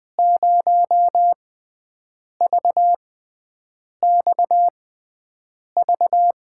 Morse code, heard as a keyed tone, about 20 wpm, spaced as for 7 wpm, 700 Hz. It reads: 0VXV
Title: USA Call Signs - 5 Characters - 6x Character Spacing - 20wpm